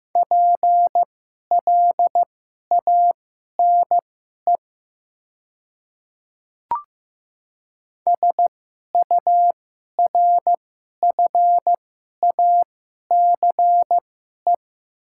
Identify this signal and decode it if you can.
Morse code, heard as a keyed tone, about 15 wpm, spaced as for 11 wpm, 700 Hz. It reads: PLANE SURFACE
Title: 2x Character Spacing - Top 500 Words - 15wpm